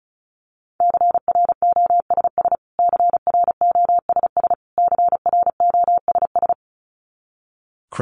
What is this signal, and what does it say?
Morse code, heard as a keyed tone, about 34 wpm, 700 Hz. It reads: CROSS CROSS CROSS
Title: Top 500 Words Repeated 35wpm